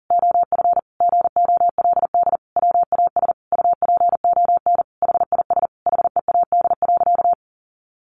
Morse code, heard as a keyed tone, about 40 wpm, 700 Hz. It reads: OF GOLD WAS UPON HIS HEAD.
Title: Sentences from Top 1100 Words 40wpm